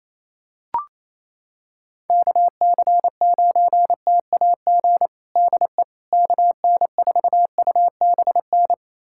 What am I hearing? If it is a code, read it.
Morse code, heard as a keyed tone, about 27 wpm, 700 Hz. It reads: KC9TAG DE KN4UBN
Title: USA Callsign DE Callsign - 6 Characters - 28wpm